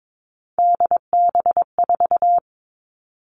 Morse code, heard as a keyed tone, about 22 wpm, 700 Hz. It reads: DB4
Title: Three Challenging Letter-Number Combinations 22wpm